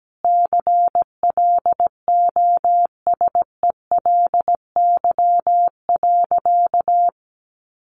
Morse code, heard as a keyed tone, about 17 wpm, 700 Hz. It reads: CLOSELY.
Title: Sentences from Top 1400 Words 17wpm